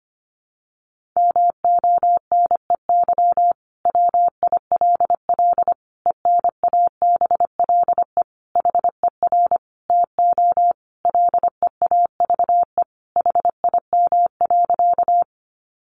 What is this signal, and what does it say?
Morse code, heard as a keyed tone, about 25 wpm, 700 Hz. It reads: MONEY WILL ENABLE HER TO LEAVE HIM.
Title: Sentences from Top 1500 Words 25wpm